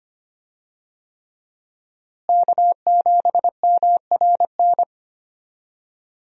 Morse code, heard as a keyed tone, about 25 wpm, 700 Hz. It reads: K7MRN
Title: USA Call Signs - 5 Characters - Repeated 3x - 25wpm